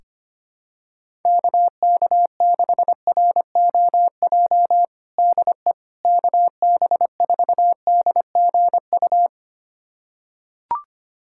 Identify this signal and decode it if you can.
Morse code, heard as a keyed tone, about 25 wpm, 700 Hz. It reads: KK6ROJ DE KB4DGU
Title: USA Callsign DE Callsign - 6 Characters - 25wpm